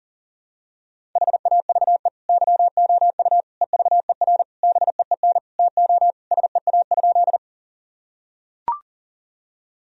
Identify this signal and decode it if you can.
Morse code, heard as a keyed tone, about 40 wpm, 700 Hz. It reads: HAVE YOU EVER BEEN TO SEA?
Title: Sentences from Top 1000 Words - Review Part 1 - 40wpm